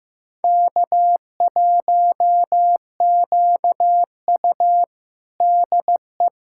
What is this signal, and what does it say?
Morse code, heard as a keyed tone, about 15 wpm, 700 Hz. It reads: K1QU DE
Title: USA Callsign DE Callsign - 4 Characters - 15wpm